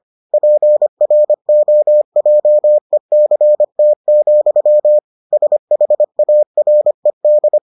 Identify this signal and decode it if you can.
Morse code, heard as a keyed tone, about 25 wpm, 600 Hz. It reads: PROJECT, SHARED